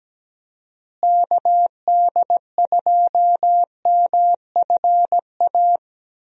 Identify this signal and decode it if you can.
Morse code, heard as a keyed tone, about 17 wpm, 700 Hz. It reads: KD2MFA
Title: USA Call Signs - 6 Characters - Repeated 3x - 17wpm